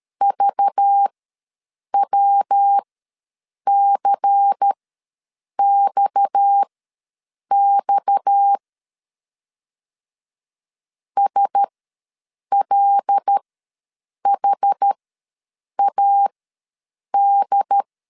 Morse code, heard as a keyed tone, about 13 wpm, spaced as for 7 wpm, 800 Hz. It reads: VWCXX SLHAD
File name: LEZIONI / 6 PKX0 / 6 PKX ws 12wpm 3106